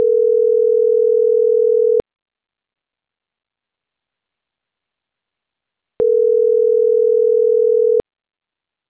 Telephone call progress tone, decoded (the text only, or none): ringback tone